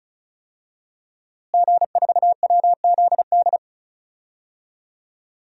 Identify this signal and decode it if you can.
Morse code, heard as a keyed tone, about 36 wpm, 700 Hz. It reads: G4WZD
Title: DXCC-100 Call Signs - 5 Characters - Repeated 2x - 35wpm